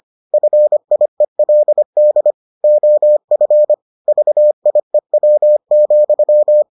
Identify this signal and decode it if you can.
Morse code, heard as a keyed tone, about 25 wpm, 600 Hz. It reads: FIELD OF VIEW,